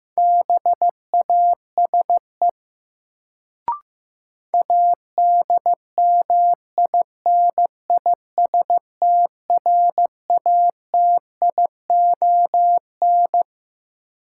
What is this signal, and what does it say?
Morse code, heard as a keyed tone, about 15 wpm, 700 Hz. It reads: BASE ADMINISTRATION
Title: Sets of 5 Words - New Word Drill - 15wpm